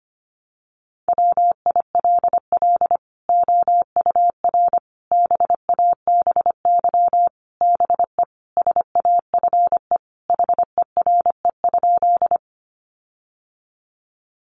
Morse code, heard as a keyed tone, about 25 wpm, 700 Hz. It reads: WILL OUR BABY BE SAFE HERE?
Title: Sentences from Top 1300 Words 25wpm